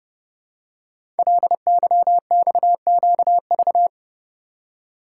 Morse code, heard as a keyed tone, about 30 wpm, 700 Hz. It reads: LYXQV
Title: Code Groups - 5 Letters - 30wpm